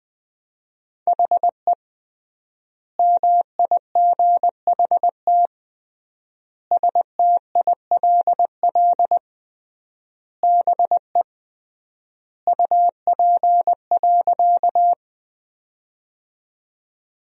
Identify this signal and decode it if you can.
Morse code, heard as a keyed tone, about 21 wpm, 700 Hz. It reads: HE MIGHT STILL BE UP.
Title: 3x Word-Spacing - Sentences from Top 300 Words - 20wpm